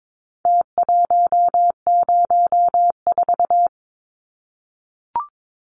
Morse code, heard as a keyed tone, about 22 wpm, 700 Hz. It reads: T104
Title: Numbers - Contest Style - 4-Digit 22wpm